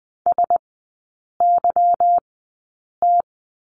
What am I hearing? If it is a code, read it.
Morse code, heard as a keyed tone, about 20 wpm, spaced as for 8 wpm, 690 Hz. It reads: SYT
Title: 145 Lesson 24e - Letter y - Three Characters 20wpm